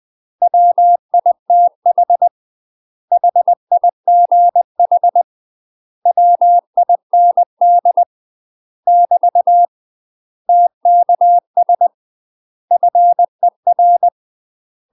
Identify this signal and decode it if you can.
Morse code, heard as a keyed tone, about 20 wpm, 700 Hz. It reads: WITH HIGH WIND <BT> TKS FER
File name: UK MASTER QSOs TRAINING / 20WPM / UK MASTER QSOs 0042